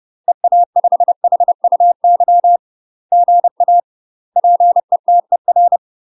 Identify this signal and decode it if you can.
Morse code, heard as a keyed tone, about 30 wpm, 700 Hz. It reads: EA5HUY GA PETER
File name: UK MASTER QSOs TRAINING / 30WPM / UK MASTER QSOs 0046